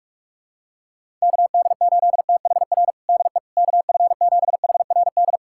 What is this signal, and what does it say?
Morse code, heard as a keyed tone, about 42 wpm, 700 Hz. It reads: KD8THR DE KF7HRD
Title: USA Callsign DE Callsign - 6 Characters - 45wpm